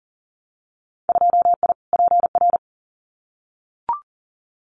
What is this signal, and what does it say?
Morse code, heard as a keyed tone, about 40 wpm, 700 Hz. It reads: 2I PR
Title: Winter Field Day Exchanges Encore 40wpm